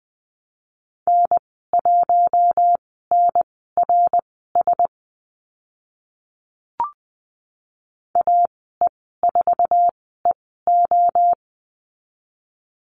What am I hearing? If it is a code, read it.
Morse code, heard as a keyed tone, about 20 wpm, spaced as for 14 wpm, 700 Hz. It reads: N1NRS AE4EO
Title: 062 Lesson 11g - Letter h - Call Signs 20wpm